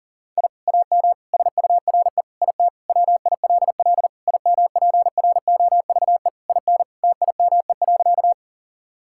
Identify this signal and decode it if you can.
Morse code, heard as a keyed tone, about 38 wpm, 700 Hz. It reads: I AM SURE IT WILL IMPROVE IN TIME.
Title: Sentences from Top 1000 Words - Review Part 5 - 40wpm